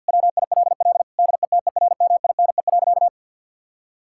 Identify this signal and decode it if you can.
Morse code, heard as a keyed tone, about 52 wpm, 700 Hz. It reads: WILL DETERMINE.
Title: Sentences from Top 1000 Words - Review Part 5 - 50wpm